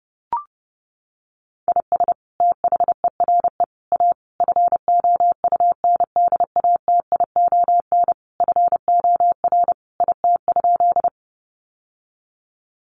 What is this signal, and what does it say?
Morse code, heard as a keyed tone, about 30 wpm, 700 Hz. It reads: IS THERE A FOUNDATION FOR IT?
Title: Sentences from Top 1200 Words 30wpm